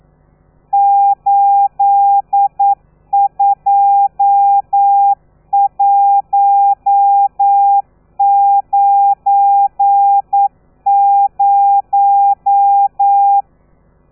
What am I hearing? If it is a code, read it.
Morse code, heard as a keyed tone, about 9 wpm, 800 Hz. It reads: 82190
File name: telegrafia / i2tfj / Lez12 40